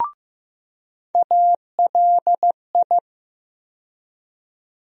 Morse code, heard as a keyed tone, about 16 wpm, 700 Hz. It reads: ALI
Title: USA Call Sign Postfixes - 3 Letters 15wpm